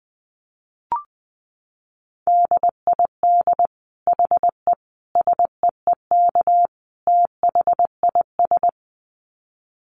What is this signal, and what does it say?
Morse code, heard as a keyed tone, about 19 wpm, 700 Hz. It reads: DID HE SEEK THIS
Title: Dit Challenge 20wpm